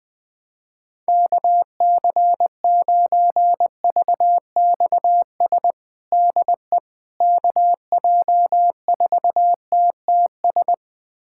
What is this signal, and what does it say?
Morse code, heard as a keyed tone, about 21 wpm, 700 Hz. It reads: KC9VXS DE KJ4TTS
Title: USA Callsign DE Callsign - 6 Characters - 20wpm